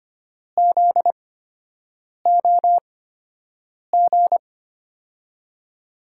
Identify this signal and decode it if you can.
Morse code, heard as a keyed tone, about 25 wpm, spaced as for 7 wpm, 700 Hz. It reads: ZOG
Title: USA Call Signs - 6 Characters - 8x Character Spacing - 25wpm